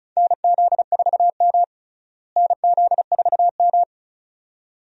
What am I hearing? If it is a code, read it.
Morse code, heard as a keyed tone, about 34 wpm, 700 Hz. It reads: NZ4M NZ4M
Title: USA Call Signs - 4 Characters - Repeated 2x - 35wpm